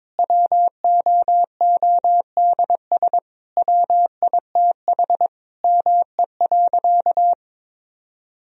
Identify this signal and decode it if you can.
Morse code, heard as a keyed tone, about 22 wpm, 700 Hz. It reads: WOODS WITH ME.
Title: Sentences from Top 1000 Words - Review Part 2 - 22wpm